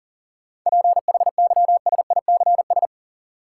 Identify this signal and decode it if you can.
Morse code, heard as a keyed tone, about 40 wpm, 700 Hz. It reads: PHYSICS
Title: Sentences from Top 1000 Words - Review Part 5 - New Word Drill - 40wpm